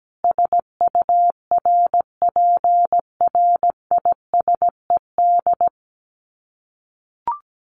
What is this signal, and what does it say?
Morse code, heard as a keyed tone, about 17 wpm, 700 Hz. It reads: SURPRISED
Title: Sets of 5 Words Encore2 17wpm